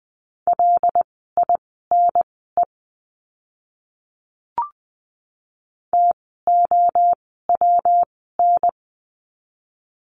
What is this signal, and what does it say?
Morse code, heard as a keyed tone, about 20 wpm, spaced as for 14 wpm, 700 Hz. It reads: LINE TOWN
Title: 115 Lesson 19f - Letter w - Words 20wpm